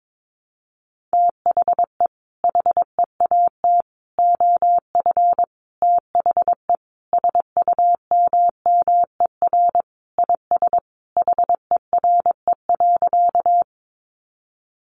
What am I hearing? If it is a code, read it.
Morse code, heard as a keyed tone, about 22 wpm, 700 Hz. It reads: THE HEAT OF THE SUMMER IS HERE.
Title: Sentences from Top 1100 Words 22wpm